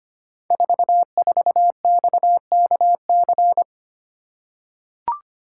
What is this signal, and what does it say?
Morse code, heard as a keyed tone, about 25 wpm, 700 Hz. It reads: 44XKC
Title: Code Groups - 5 Letter:Numbers - 25wpm